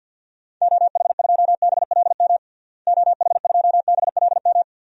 Morse code, heard as a keyed tone, about 50 wpm, 700 Hz. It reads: OH2BLM OH2BLM
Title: DXCC-100 Call Signs - 6 Characters - Repeated 2x - 50wpm